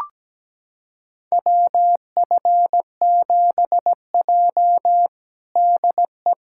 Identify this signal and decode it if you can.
Morse code, heard as a keyed tone, about 17 wpm, 700 Hz. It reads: WF7J DE
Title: USA Callsign DE Callsign - 4 Characters - 17wpm